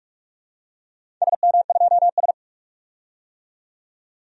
Morse code, heard as a keyed tone, about 46 wpm, 700 Hz. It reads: SM2S
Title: DXCC-100 Call Signs - 4 Characters - 45wpm